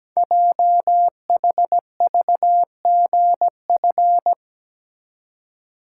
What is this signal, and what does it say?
Morse code, heard as a keyed tone, about 17 wpm, 700 Hz. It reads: JHVGF
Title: Code Groups - 5 Letters - 17wpm